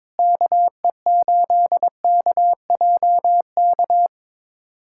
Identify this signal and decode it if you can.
Morse code, heard as a keyed tone, about 22 wpm, 700 Hz. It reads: KE8KJK
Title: USA Callsign DE Callsign 22wpm